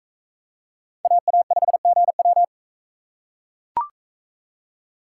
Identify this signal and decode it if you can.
Morse code, heard as a keyed tone, about 39 wpm, 700 Hz. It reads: AA5GW